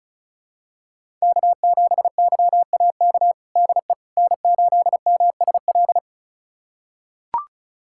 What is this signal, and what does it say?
Morse code, heard as a keyed tone, about 34 wpm, 700 Hz. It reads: K7YAK DE N8MSL
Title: USA Callsign DE Callsign - 5 Characters - 35wpm